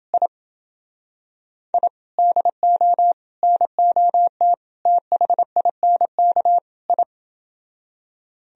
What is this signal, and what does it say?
Morse code, heard as a keyed tone, about 28 wpm, 700 Hz. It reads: I I DO NOT THINK I